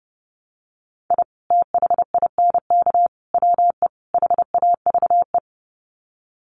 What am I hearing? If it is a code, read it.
Morse code, heard as a keyed tone, about 30 wpm, 700 Hz. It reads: I THINK WE HAVE
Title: Sets of 4 Words 30wpm